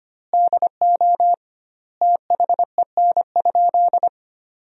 Morse code, heard as a keyed tone, about 25 wpm, 700 Hz. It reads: DO THEN?
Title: 2x Word-Spacing - Sentences from Top 100 Words - 25wpm